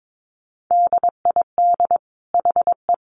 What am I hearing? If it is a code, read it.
Morse code, heard as a keyed tone, about 22 wpm, 700 Hz. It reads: DID HE
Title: Sentences from Top 1300 Words 22wpm